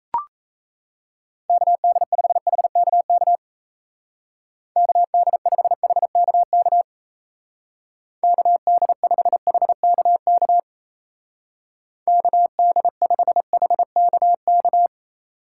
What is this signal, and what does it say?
Morse code, heard as a keyed tone, about 34 wpm, 700 Hz. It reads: KD5HKK KD5HKK KD5HKK KD5HKK
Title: Speed Racing - USA 2x3 Call Signs 28wpm